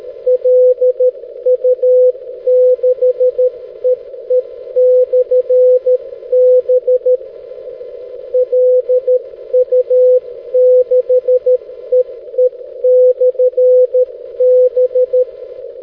Morse code, heard as a keyed tone, about 13 wpm, 500 Hz. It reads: LU6EE/B LU6EE/B
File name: 6m-lu6ee-b